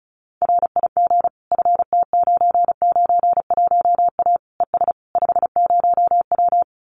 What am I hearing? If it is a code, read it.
Morse code, heard as a keyed tone, about 34 wpm, 700 Hz. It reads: RIG FT991A ES 50W
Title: Common QSO Phrases Encore 35wpm